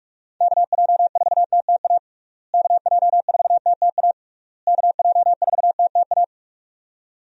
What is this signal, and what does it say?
Morse code, heard as a keyed tone, about 46 wpm, 700 Hz. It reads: KJ4TTA KJ4TTA KJ4TTA
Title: USA Call Signs - 6 Characters - Repeated 3x - 45wpm